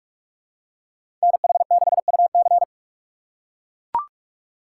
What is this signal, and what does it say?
Morse code, heard as a keyed tone, about 43 wpm, 700 Hz. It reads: NH6UC
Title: USA Call Signs - 4x Character Spacing - 45wpm